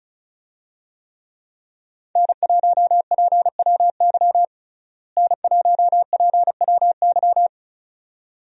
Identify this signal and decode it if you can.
Morse code, heard as a keyed tone, about 36 wpm, 700 Hz. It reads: N1PWY N1PWY
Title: USA Call Signs - 5 Characters - Repeated 2x - 35wpm